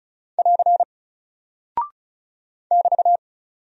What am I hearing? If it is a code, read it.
Morse code, heard as a keyed tone, about 36 wpm, 700 Hz. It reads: <AR> <BT>
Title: Single Character (Letter:Number:Punctuation:Prosign) 35wpm